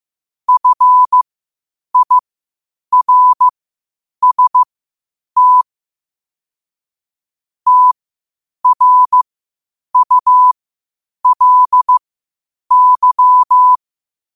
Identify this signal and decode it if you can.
Morse code, heard as a keyed tone, about 15 wpm, spaced as for 8 wpm, 1000 Hz. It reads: FIRST TRULY
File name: qotdfiles / 8wpm